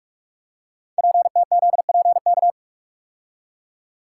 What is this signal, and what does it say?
Morse code, heard as a keyed tone, about 44 wpm, 690 Hz. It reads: PTZPK
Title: Code Groups - 5 Letters - 45wpm